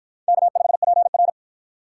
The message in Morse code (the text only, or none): K5PR